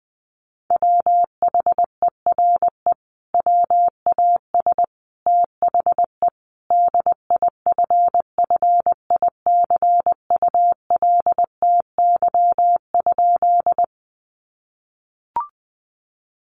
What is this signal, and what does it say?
Morse code, heard as a keyed tone, about 20 wpm, 700 Hz. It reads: WHERE WAS THE DIFFICULTY?
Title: Sentences from Top 1500 Words 20wpm